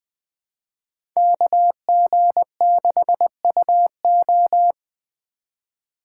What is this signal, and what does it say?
Morse code, heard as a keyed tone, about 20 wpm, 700 Hz. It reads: KG6UO
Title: USA Call Signs - 2x Character Spacing - 20wpm